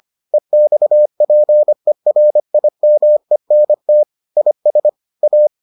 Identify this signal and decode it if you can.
Morse code, heard as a keyed tone, about 25 wpm, 600 Hz. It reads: EXPERIMENT IS A